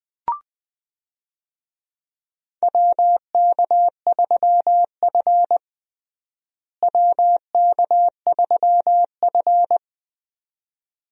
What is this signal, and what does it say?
Morse code, heard as a keyed tone, about 20 wpm, 700 Hz. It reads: WK3F WK3F
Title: USA Call Signs - 4 Characters - Repeated 2x - 20wpm